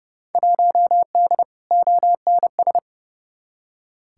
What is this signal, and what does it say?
Morse code, heard as a keyed tone, about 30 wpm, 700 Hz. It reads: 1D ONS
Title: Field Day Exchanges - 30wpm